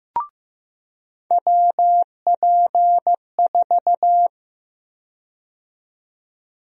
Morse code, heard as a keyed tone, about 15 wpm, 700 Hz. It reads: WP4
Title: USA Call Sign Prefixes - 15wpm